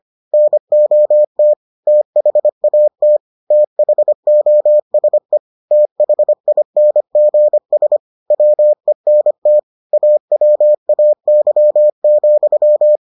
Morse code, heard as a keyed tone, about 25 wpm, 600 Hz. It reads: NOT THAT THOSE THINGS WENT AWAY,